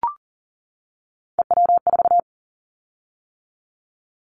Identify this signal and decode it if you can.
Morse code, heard as a keyed tone, about 38 wpm, 700 Hz. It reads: EW4
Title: Callsign Prefixes for Top 50 DXCC - 40wpm